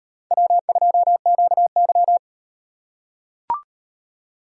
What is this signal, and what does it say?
Morse code, heard as a keyed tone, about 35 wpm, 700 Hz. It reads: W2QY